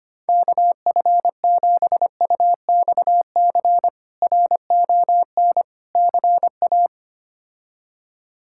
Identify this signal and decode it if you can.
Morse code, heard as a keyed tone, about 25 wpm, 700 Hz. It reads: KF7UXC RON CA
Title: K1USN SST QSO Practice - 25wpm